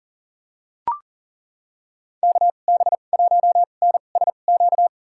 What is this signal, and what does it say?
Morse code, heard as a keyed tone, about 40 wpm, spaced as for 29 wpm, 700 Hz. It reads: KB1NSQ
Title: USA Call Signs - 2x Character Spacing - 40wpm